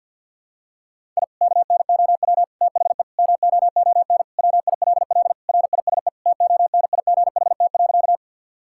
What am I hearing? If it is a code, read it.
Morse code, heard as a keyed tone, about 51 wpm, 700 Hz. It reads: I KNOW THE MOON WILL RISE TONIGHT.
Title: Sentences from Top 1000 Words - Review Part 6 - 50wpm